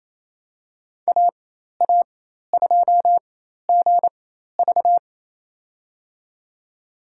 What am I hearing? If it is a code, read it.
Morse code, heard as a keyed tone, about 27 wpm, spaced as for 13 wpm, 700 Hz. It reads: AA2GV
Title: USA Call Signs - 5 Characters - 4x Character Spacing - 28wpm